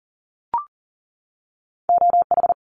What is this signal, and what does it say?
Morse code, heard as a keyed tone, about 40 wpm, 700 Hz. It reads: OH
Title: Top 500 Words 40wpm